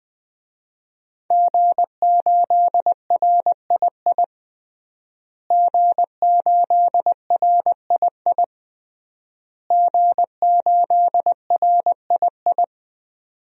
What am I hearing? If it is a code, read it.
Morse code, heard as a keyed tone, about 20 wpm, 700 Hz. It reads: G8RII G8RII G8RII
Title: DXCC-100 Call Signs - 5 Characters - Repeated 3x - 20wpm